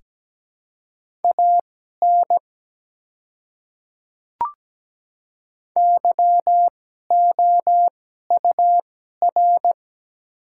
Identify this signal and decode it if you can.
Morse code, heard as a keyed tone, about 17 wpm, spaced as for 12 wpm, 700 Hz. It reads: AN YOUR
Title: 2x Character Spacing - Top 300 Words - 17wpm